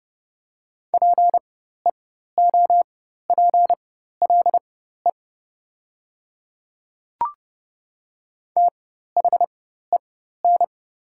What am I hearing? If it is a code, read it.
Morse code, heard as a keyed tone, about 30 wpm, spaced as for 14 wpm, 700 Hz. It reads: PEOPLE THEN
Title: 4x Character Spacing - Top 200 Words - 30wpm